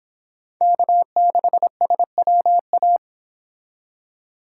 Morse code, heard as a keyed tone, about 26 wpm, 700 Hz. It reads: K6SWA